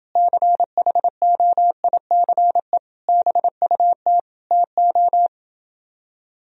Morse code, heard as a keyed tone, about 29 wpm, 700 Hz. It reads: CHOICE BUT TO